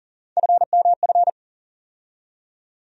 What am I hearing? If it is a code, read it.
Morse code, heard as a keyed tone, about 40 wpm, 700 Hz. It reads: FMF
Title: USA Call Sign Postfixes - 3 Letters 40wpm